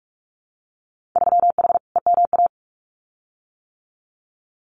Morse code, heard as a keyed tone, about 46 wpm, 700 Hz. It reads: 3H EMA
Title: Winter Field Day Exchanges Encore 45wpm